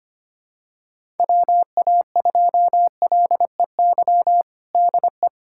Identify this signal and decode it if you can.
Morse code, heard as a keyed tone, about 25 wpm, 700 Hz. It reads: WA2LEY DE
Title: USA Callsign DE Callsign 25wpm